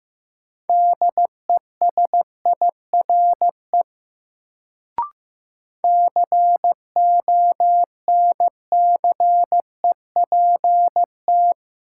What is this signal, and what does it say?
Morse code, heard as a keyed tone, about 15 wpm, 700 Hz. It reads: DESIRE CONCEPT
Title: Sentences from Top 1100 Words - New Word Drill - 15wpm